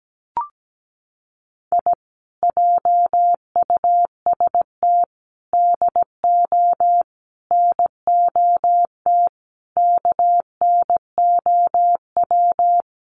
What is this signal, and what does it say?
Morse code, heard as a keyed tone, about 17 wpm, 700 Hz. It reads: I JUST DO NOT KNOW
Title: Sets of 5 Words Encore 17wpm